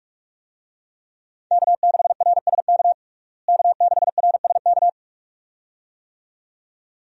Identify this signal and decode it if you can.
Morse code, heard as a keyed tone, about 46 wpm, 700 Hz. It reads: K6RSK K6RSK
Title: USA Call Signs - 5 Characters - Repeated 2x - 45wpm